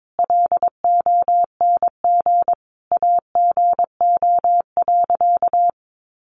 Morse code, heard as a keyed tone, about 22 wpm, 700 Hz. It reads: LONG AGO.
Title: Sentences from Top 500 Words 22wpm